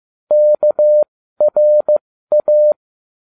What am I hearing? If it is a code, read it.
Morse code, heard as a keyed tone, about 15 wpm, spaced as for 13 wpm, 600 Hz. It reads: KRA